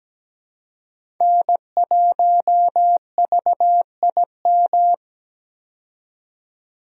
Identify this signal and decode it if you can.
Morse code, heard as a keyed tone, about 17 wpm, 700 Hz. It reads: N1VIM
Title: USA Call Signs - 5 Characters - 17wpm